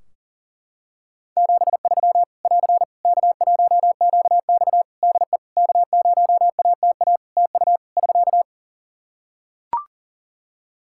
Morse code, heard as a keyed tone, about 40 wpm, 700 Hz. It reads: 73 <AR> K1QX DE K0ATA TU <SK>
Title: Practice QSO - Exchange 4 - Ending - 40wpm